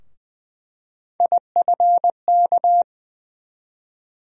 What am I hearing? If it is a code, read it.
Morse code, heard as a keyed tone, about 20 wpm, 700 Hz. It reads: IFK
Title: USA Call Sign Postfixes - 3 Letters 20wpm